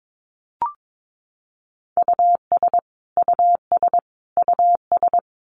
Morse code, heard as a keyed tone, about 22 wpm, 700 Hz. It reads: US US US
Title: Top 300 Words Repeated 22wpm